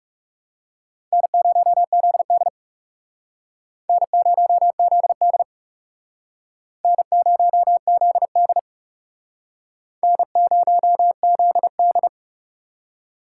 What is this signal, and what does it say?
Morse code, heard as a keyed tone, about 38 wpm, 700 Hz. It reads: N0ZD N0ZD N0ZD N0ZD
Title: Speed Racing - USA 1x2 Call Signs 30wpm